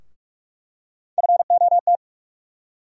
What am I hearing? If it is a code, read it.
Morse code, heard as a keyed tone, about 46 wpm, 700 Hz. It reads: FOT
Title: USA Call Sign Postfixes - 3 Letters 45wpm